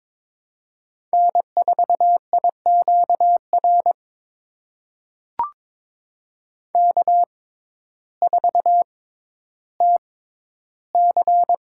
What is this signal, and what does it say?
Morse code, heard as a keyed tone, about 22 wpm, spaced as for 12 wpm, 700 Hz. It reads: N4IQR K4TC
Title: USA Call Signs - 6x Character Spacing - 22wpm